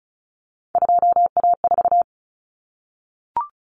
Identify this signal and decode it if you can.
Morse code, heard as a keyed tone, about 34 wpm, 700 Hz. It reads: 2A4
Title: Common Cut Numbers - 3-Digit 35wpm